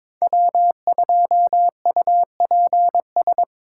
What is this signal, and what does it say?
Morse code, heard as a keyed tone, about 22 wpm, 700 Hz. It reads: W2UPS
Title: USA Call Signs - 5 Characters - 4x Character Spacing - 22wpm